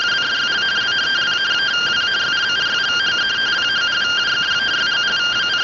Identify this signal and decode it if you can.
Two-tone teleprinter signal, about 170 Hz shift.
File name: findings / CT 205 ax mips, iDose 3 0007.gen-14 rtty rx